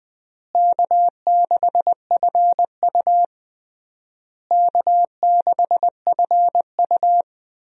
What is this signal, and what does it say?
Morse code, heard as a keyed tone, about 20 wpm, 700 Hz. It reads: K6FU K6FU
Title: USA Call Signs - 4 Characters - Repeated 3x - 20wpm